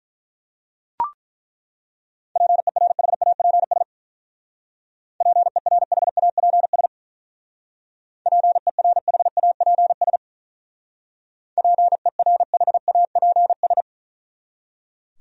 Morse code, heard as a keyed tone, about 42 wpm, 700 Hz. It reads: PERHAPS PERHAPS PERHAPS PERHAPS
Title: Speed-Racing Top 500 Words 35wpm